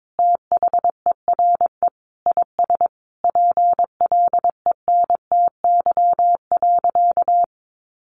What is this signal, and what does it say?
Morse code, heard as a keyed tone, about 22 wpm, 700 Hz. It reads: THERE IS PLENTY.
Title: Sentences from Top 1300 Words 22wpm